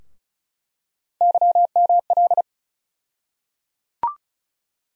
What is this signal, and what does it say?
Morse code, heard as a keyed tone, about 34 wpm, 700 Hz. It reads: YML
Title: USA Call Sign Postfixes - 3 Letters 35wpm